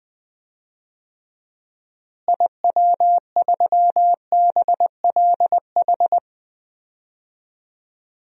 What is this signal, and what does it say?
Morse code, heard as a keyed tone, about 21 wpm, 700 Hz. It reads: IW3BLH
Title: DXCC-100 Call Signs - 6 Characters - Repeated 3x - 20wpm